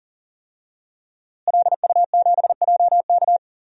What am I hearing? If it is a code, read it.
Morse code, heard as a keyed tone, about 39 wpm, 700 Hz. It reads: LU7JK
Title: DXCC-100 Call Signs - 5 Characters - Repeated 2x - 40wpm